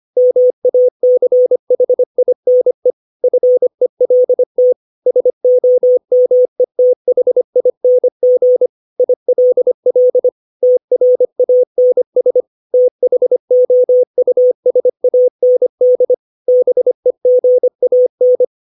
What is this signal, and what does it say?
Morse code, heard as a keyed tone, about 25 wpm, 500 Hz. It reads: MACHINE FELT SOMETHING ILL TRANS THOUSAND BEGAN
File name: 632 words / 632 Words 25WPM